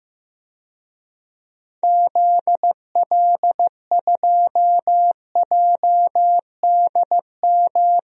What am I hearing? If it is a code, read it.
Morse code, heard as a keyed tone, about 15 wpm, 700 Hz. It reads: ZL2JDM
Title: DXCC-100 Call Signs - 6 Characters - 15wpm